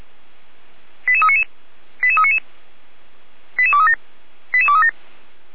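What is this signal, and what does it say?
Five-tone selective call (ZVEI), heard as quick runs of five tones, tones of about 70 ms. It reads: 88399 88399 88237 88237
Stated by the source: ZVEI five-tone selective call received by SDR, on the server of the Orlamünde fire brigade (Feuerwehr Orlamünde).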